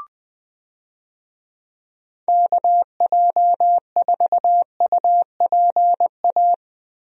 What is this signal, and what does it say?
Morse code, heard as a keyed tone, about 20 wpm, 700 Hz. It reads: KJ4UPA
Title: USA Call Signs - 6 Characters - Repeated 2x - 20wpm